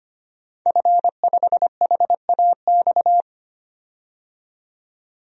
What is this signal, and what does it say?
Morse code, heard as a keyed tone, about 25 wpm, 700 Hz. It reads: F5HAX
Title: DXCC-100 Call Signs - 5 Characters - 25wpm